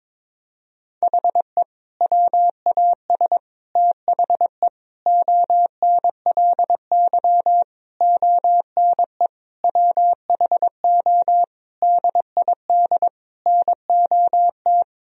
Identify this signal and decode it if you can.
Morse code, heard as a keyed tone, about 22 wpm, 700 Hz. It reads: HE WAS THE ONLY ONE WHO DID NOT
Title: Sentences from Top 1000 Words - Review Part 3 - 22wpm